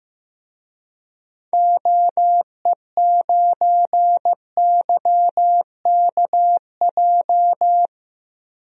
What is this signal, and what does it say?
Morse code, heard as a keyed tone, about 15 wpm, 700 Hz. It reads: OE9YKJ
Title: DXCC-100 Call Signs - 6 Characters - 15wpm